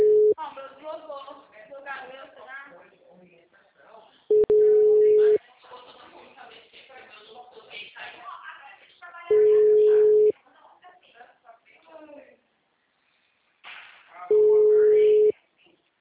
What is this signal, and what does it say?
Ringback tone (Europe, 425 Hz), with speech.